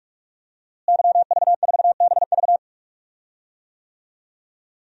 Morse code, heard as a keyed tone, about 41 wpm, 700 Hz. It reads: YV4BV
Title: DXCC-100 Call Signs - 5 Characters - Repeated 3x - 45wpm